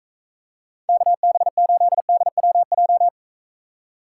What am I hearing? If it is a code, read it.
Morse code, heard as a keyed tone, about 43 wpm, 700 Hz. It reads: KB8DWJ